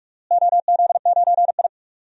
Morse code, heard as a keyed tone, about 41 wpm, 700 Hz. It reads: OZ9I
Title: DXCC-100 Call Signs - 4 Characters - Repeated 3x - 45wpm